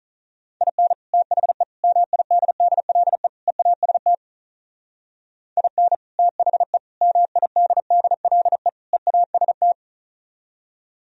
Morse code, heard as a keyed tone, about 46 wpm, 700 Hz. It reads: IN THE MIDDLE EAST IN THE MIDDLE EAST